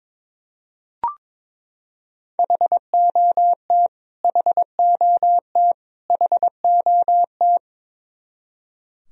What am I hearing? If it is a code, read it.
Morse code, heard as a keyed tone, about 22 wpm, 700 Hz. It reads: HOT HOT HOT
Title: Top 100 Words Repeated 22wpm